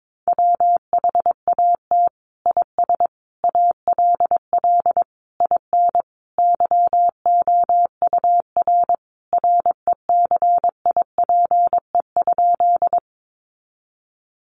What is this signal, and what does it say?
Morse code, heard as a keyed tone, about 22 wpm, 700 Hz. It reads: WHAT IS ALL IN YOUR RECIPE?
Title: Sentences from Top 1200 Words 22wpm